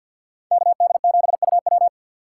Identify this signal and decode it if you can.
Morse code, heard as a keyed tone, about 48 wpm, 700 Hz. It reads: KD7UW
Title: USA Call Signs - 5 Characters - 3x Character Spacing - 50wpm